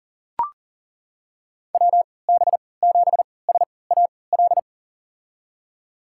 Morse code, heard as a keyed tone, about 40 wpm, spaced as for 23 wpm, 700 Hz. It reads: WB7SAL
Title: USA Call Signs - 6 Characters - 3x Character Spacing - 40wpm